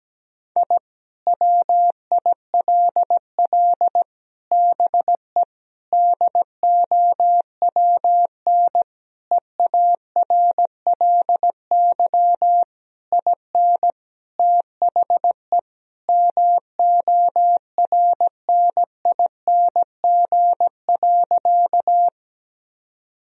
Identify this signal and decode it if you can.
Morse code, heard as a keyed tone, about 17 wpm, 700 Hz. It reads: I WILL BE DOWN EARLY IN THE MORNING.
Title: Sentences from Top 1000 Words - Review Part 2 - 17wpm